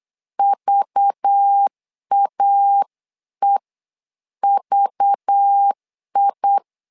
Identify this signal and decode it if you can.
Morse code, heard as a keyed tone, about 8 wpm, spaced as for 7 wpm, 800 Hz. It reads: VAEVI
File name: LEZIONI / 5 LBVC / 5 LBV 8wpm 2837